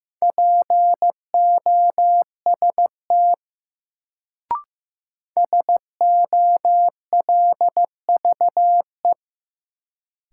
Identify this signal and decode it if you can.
Morse code, heard as a keyed tone, about 15 wpm, 700 Hz. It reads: POST SOLVE
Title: Sentences from Top 1000 Words - New Word Drill - 15wpm